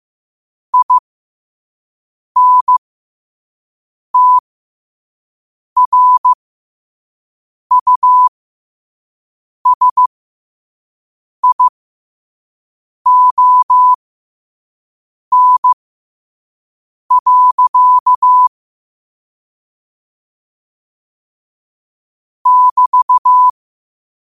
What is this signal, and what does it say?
Morse code, heard as a keyed tone, about 15 wpm, spaced as for 5 wpm, 1000 Hz. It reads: INTRUSION. <BT>
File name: qotdfiles / 5wpm